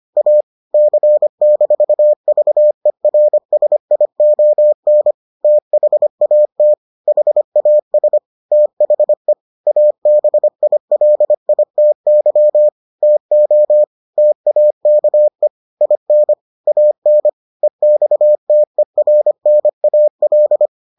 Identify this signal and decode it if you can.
Morse code, heard as a keyed tone, about 25 wpm, 600 Hz. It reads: A C-VERSION THAT HAS THE ABILITY TO TAKE IN AN EXTERNAL